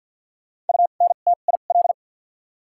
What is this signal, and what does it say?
Morse code, heard as a keyed tone, about 52 wpm, spaced as for 37 wpm, 700 Hz. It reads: UNTIL
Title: 2x Character Spacing - Top 300 Words - 50wpm